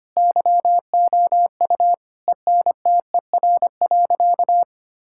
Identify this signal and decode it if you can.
Morse code, heard as a keyed tone, about 25 wpm, 700 Hz. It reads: YOU ENTER.
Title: Sentences from Top 1600 Words 25wpm